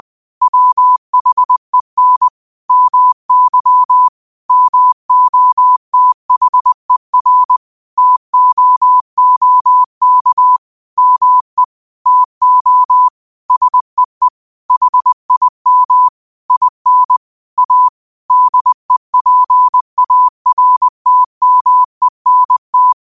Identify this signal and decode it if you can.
Morse code, heard as a keyed tone, about 20 wpm, 1000 Hz. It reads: WHEN MY MOTHER TOOK ME TO SEE HIM IN A DEPARTMENT
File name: qotdfiles / 20wpm